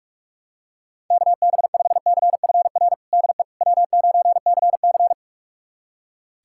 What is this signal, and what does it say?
Morse code, heard as a keyed tone, about 42 wpm, 700 Hz. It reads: KB5CFR DE W9CC
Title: USA Callsign DE Callsign 45wpm